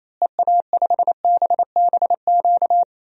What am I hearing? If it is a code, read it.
Morse code, heard as a keyed tone, about 28 wpm, 700 Hz. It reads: EA5BBQ
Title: DXCC-100 Call Signs - 6 Characters - Repeated 2x - 28wpm